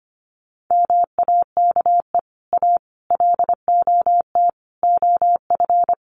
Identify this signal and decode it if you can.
Morse code, heard as a keyed tone, about 25 wpm, 700 Hz. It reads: MAKE A LOT OF
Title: Sets of 5 Words Encore2 25wpm